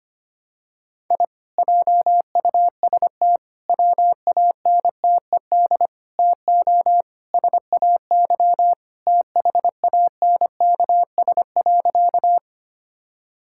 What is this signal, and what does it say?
Morse code, heard as a keyed tone, about 25 wpm, 700 Hz. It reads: I JUST WANTED TO SAY THANKS.
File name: practice / 25wpm / Sentences with 4-Word Phrases - 25wpm